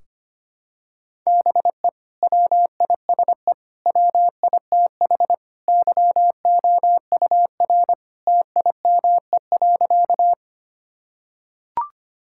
Morse code, heard as a keyed tone, about 25 wpm, 700 Hz. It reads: BE WISE WITH YOUR TIME.
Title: Sentences from Top 1600 Words 25wpm